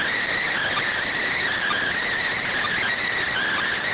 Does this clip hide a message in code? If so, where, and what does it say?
Yes: Emergency Alert System (SAME) tones from 0.3 to 3.1 s, heard as harsh buzzing bursts.